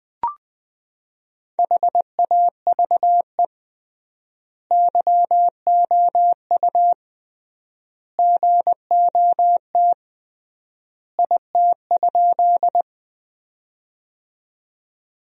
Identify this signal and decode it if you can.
Morse code, heard as a keyed tone, about 20 wpm, 700 Hz. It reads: HAVE YOU GOT IT?
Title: 3x Word-Spacing - Sentences from Top 300 Words - 20wpm